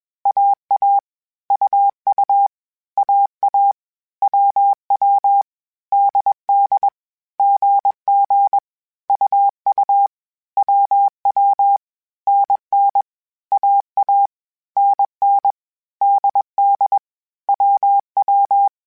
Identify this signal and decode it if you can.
Morse code, heard as a keyed tone, about 21 wpm, 800 Hz. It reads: AA UU AA WW DD GG UU WW NN AA NN DD WW